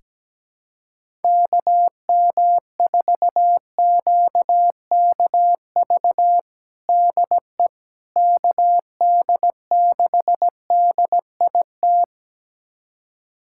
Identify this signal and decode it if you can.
Morse code, heard as a keyed tone, about 17 wpm, 700 Hz. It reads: KM4QKV DE KD6DIT
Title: USA Callsign DE Callsign - 6 Characters - 17wpm